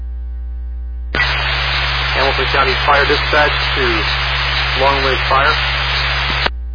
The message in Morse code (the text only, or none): none